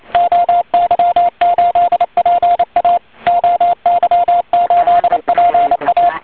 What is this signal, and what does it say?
Morse code, heard as a keyed tone, about 28 wpm, 700 Hz, with radio noise. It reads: OY8PA OY8PA
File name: oy 2003 / real-audio / 160m-oy8pa-cw-10.5min